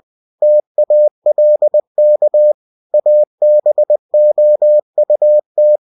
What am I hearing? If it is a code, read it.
Morse code, heard as a keyed tone, about 20 wpm, 600 Hz. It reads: TALK ABOUT
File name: cw-books / winnie-the-pooh / winnie-20wpm-page-0103